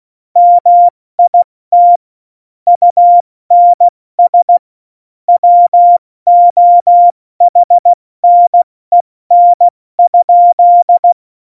Morse code, heard as a keyed tone, about 16 wpm, spaced as for 14 wpm, 700 Hz. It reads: MIT UNS WOHNEN?
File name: Morsekurs / Witze / Witz 21. Januar 2020, 70 F80